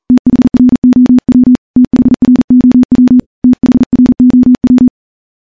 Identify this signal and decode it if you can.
Morse code, heard as a keyed tone, about 40 wpm, 250 Hz. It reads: THROW THROW THROW
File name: morse / morse-training-40WPM